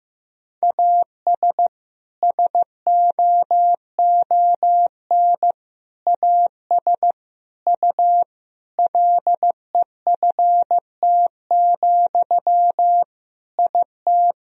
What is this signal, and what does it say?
Morse code, heard as a keyed tone, about 15 wpm, 700 Hz. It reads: AS SOON AS U LEFT, IT
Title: Sentences from Top 400 Words and Abbreviations 15wpm